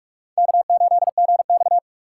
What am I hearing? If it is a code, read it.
Morse code, heard as a keyed tone, about 44 wpm, 700 Hz. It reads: K8GX
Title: USA Call Signs - 4 Characters - 3x Character Spacing - 45wpm